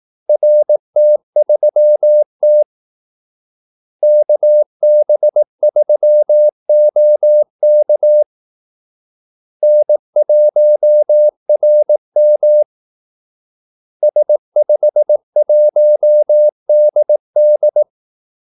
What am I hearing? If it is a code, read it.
Morse code, heard as a keyed tone, about 18 wpm, 600 Hz. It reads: RT3T KB3OK N1RM S51DD
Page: Kaldesignaler 18 wpm